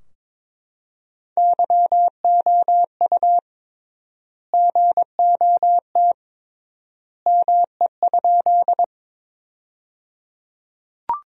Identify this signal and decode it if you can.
Morse code, heard as a keyed tone, about 22 wpm, 700 Hz. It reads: YOU GOT ME?
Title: 3x Word-Spacing - Sentences from Top 300 Words - 22wpm